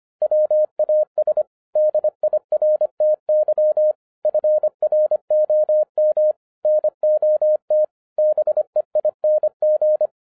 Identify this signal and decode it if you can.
Morse code, heard as a keyed tone, about 25 wpm, 610 Hz. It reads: WAS DIRTY FROM NOT BEING
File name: FIRST FISTS CONTACT-25wpm